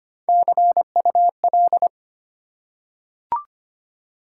Morse code, heard as a keyed tone, about 25 wpm, 700 Hz. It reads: CUL
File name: Top 90 QSO Elements - 3x Character Spacing - 25wpm